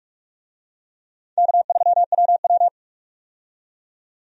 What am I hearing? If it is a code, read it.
Morse code, heard as a keyed tone, about 47 wpm, 700 Hz. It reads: K3WW
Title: USA Call Signs - 4 Characters - 45wpm